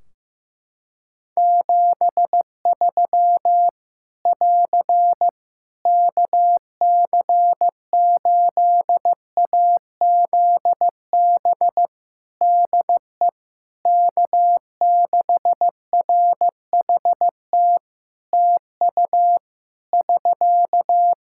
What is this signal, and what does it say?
Morse code, heard as a keyed tone, about 15 wpm, 700 Hz. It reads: 73 <AR> KC8AZB DE K6RHT TU <SK>
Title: Practice QSO - Exchange 4 - Ending - 15wpm